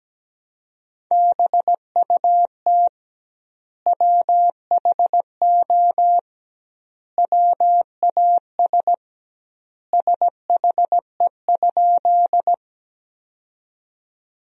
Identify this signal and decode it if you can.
Morse code, heard as a keyed tone, about 17 wpm, 700 Hz. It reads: BUT WHO WAS SHE?
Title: 2x Word-Spacing - Sentences from Top 100 Words - 17wpm